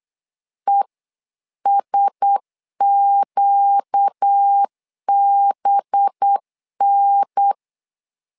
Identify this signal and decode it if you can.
Morse code, heard as a keyed tone, about 8 wpm, 800 Hz. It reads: ESQBN
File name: LEZIONI / 5 LBVC / 5 LBV 8wpm 2837